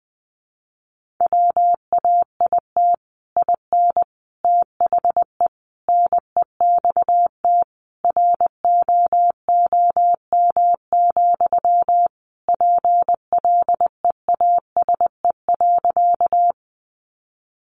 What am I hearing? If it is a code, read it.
Morse code, heard as a keyed tone, about 20 wpm, 700 Hz. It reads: WAIT IN THE NEXT ROOM, PLEASE.
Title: Sentences from Top 1500 Words 20wpm